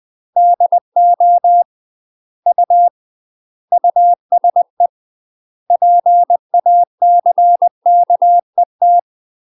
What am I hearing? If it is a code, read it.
Morse code, heard as a keyed tone, about 20 wpm, 700 Hz. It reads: DO U USE PACKET
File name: UK MASTER QSOs TRAINING / 20WPM / UK MASTER QSOs 0055